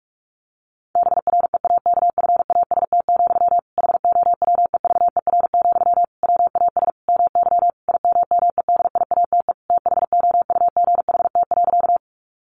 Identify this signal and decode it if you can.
Morse code, heard as a keyed tone, about 46 wpm, 700 Hz. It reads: BREAKFAST, HOWEVER, WAS MY IMMEDIATE THOUGHT.
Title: Sentences from Top 1500 Words 45wpm